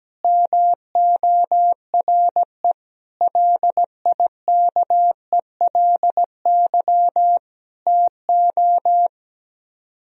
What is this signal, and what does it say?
Morse code, heard as a keyed tone, about 17 wpm, 700 Hz. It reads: MORE LIKELY TO
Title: Speed-Racing - Sets of 4 Words 17wpm